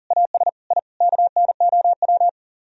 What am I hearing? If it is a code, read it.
Morse code, heard as a keyed tone, about 49 wpm, 700 Hz. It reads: AS I KNOW
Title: Speed-Racing - Sets of 5 Words 40wpm